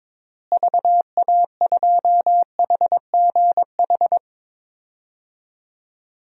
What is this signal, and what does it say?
Morse code, heard as a keyed tone, about 22 wpm, 700 Hz. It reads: VA2HGH
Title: DXCC-100 Call Signs - 6 Characters - Repeated 2x - 22wpm